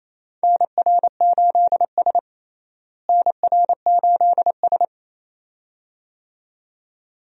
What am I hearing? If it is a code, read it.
Morse code, heard as a keyed tone, about 28 wpm, 700 Hz. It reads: NR8S NR8S
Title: USA Call Signs - 4 Characters - Repeated 2x - 28wpm